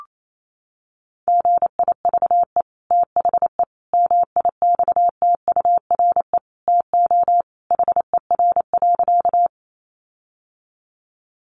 Morse code, heard as a keyed tone, about 28 wpm, 700 Hz. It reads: GIVE THE MIXTURE TO HER.
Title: Sentences from Top 1500 Words 28wpm